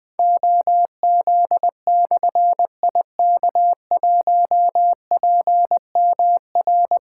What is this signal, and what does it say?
Morse code, heard as a keyed tone, about 21 wpm, 700 Hz. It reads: OZ/IK1PMR
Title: Difficult Call Signs 20wpm